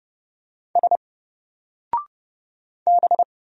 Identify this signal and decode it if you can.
Morse code, heard as a keyed tone, about 30 wpm, 700 Hz. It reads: S B
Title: Single Character (Letter:Number:Punctuation:Prosign) 30wpm